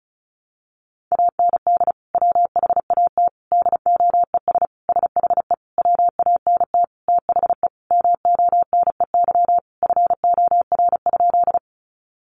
Morse code, heard as a keyed tone, about 34 wpm, 700 Hz. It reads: AND WHAT DOES SHE WANT THE MONEY FOR?
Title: Sentences from Top 400 Words 35wpm